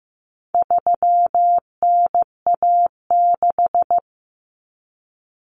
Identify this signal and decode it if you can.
Morse code, heard as a keyed tone, about 15 wpm, 700 Hz. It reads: 3NA6
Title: Common Cut Numbers - 4-Digit 15wpm